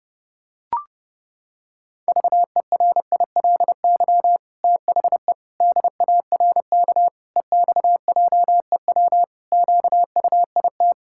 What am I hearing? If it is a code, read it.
Morse code, heard as a keyed tone, about 30 wpm, 700 Hz. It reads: VERILY THE DARK EXJEW QUIT
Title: Pangram Challenge v2 - 30wpm